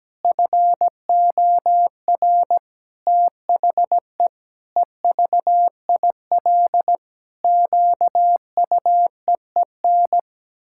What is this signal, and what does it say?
Morse code, heard as a keyed tone, about 17 wpm, 700 Hz. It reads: FOR THE EVIL QUEEN
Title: Pangram Challenge v2 - 17wpm